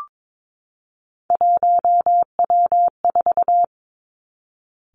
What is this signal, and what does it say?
Morse code, heard as a keyed tone, about 22 wpm, 700 Hz. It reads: 1W4
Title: Three Challenging Letter-Number Combinations 22wpm